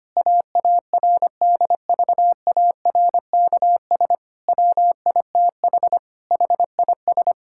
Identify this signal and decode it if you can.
Morse code, heard as a keyed tone, about 25 wpm, 700 Hz. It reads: AARDVARKS WITH HIS
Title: Pangram Challenge v2 - 25wpm